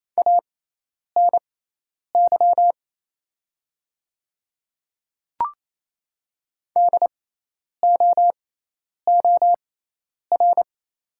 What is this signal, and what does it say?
Morse code, heard as a keyed tone, about 27 wpm, spaced as for 10 wpm, 690 Hz. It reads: ANY DOOR
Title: 6x Character Spacing - Top 300 Words - 28wpm